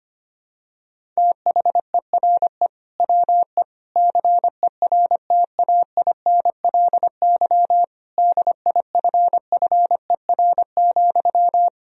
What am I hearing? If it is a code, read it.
Morse code, heard as a keyed tone, about 25 wpm, 700 Hz. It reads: THERE WE CERTAINLY DIFFER,
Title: Sentences from Top 1000 Words - Review Part 1 - 25wpm